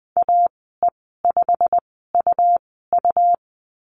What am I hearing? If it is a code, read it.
Morse code, heard as a keyed tone, about 20 wpm, spaced as for 15 wpm, 700 Hz. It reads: AE5UU
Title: 186 Lesson 30g - Letter b - Call Signs 20wpm